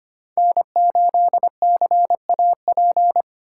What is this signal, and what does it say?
Morse code, heard as a keyed tone, about 25 wpm, 700 Hz. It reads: N8CAP
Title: USA Call Signs - 2x Character Spacing - 25wpm